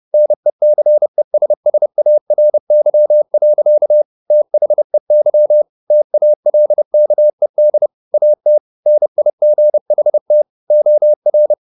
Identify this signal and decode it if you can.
Morse code, heard as a keyed tone, about 30 wpm, 600 Hz. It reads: NECESSARY. THEY TALKED AT NIGHT OR
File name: the old man and the sea / the old man and the sea 1 30wpm